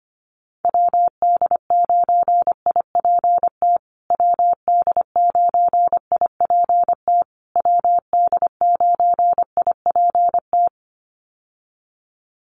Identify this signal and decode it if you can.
Morse code, heard as a keyed tone, about 25 wpm, 700 Hz. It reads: WD9IPT WD9IPT WD9IPT
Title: USA Call Signs Repeated 3x 25wpm